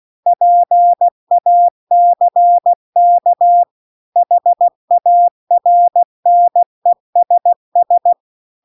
Morse code, heard as a keyed tone, about 16 wpm, 700 Hz. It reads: PACK HARNESS